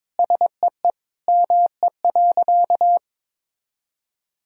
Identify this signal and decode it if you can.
Morse code, heard as a keyed tone, about 22 wpm, 700 Hz. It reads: SEE ME.
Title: Sentences from Top 1000 Words - Review Part 1 - 22wpm